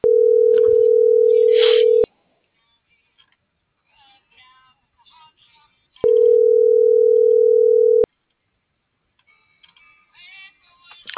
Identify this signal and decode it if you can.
Ringback tone (North America), 440 + 480 Hz.